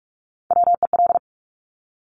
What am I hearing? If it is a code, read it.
Morse code, heard as a keyed tone, about 46 wpm, 700 Hz. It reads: WEL
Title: Top Three Letter Combinations in Words 45wpm